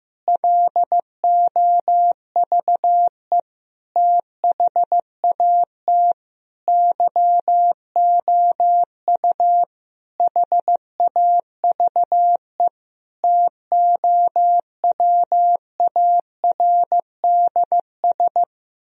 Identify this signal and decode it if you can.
Morse code, heard as a keyed tone, about 15 wpm, 700 Hz. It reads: LOVE THAT YOU HAVE TOWARDS
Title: Sentences from Top 1000 Words - Review Part 4 - 15wpm